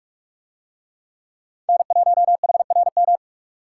Morse code, heard as a keyed tone, about 40 wpm, 700 Hz. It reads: N1HRM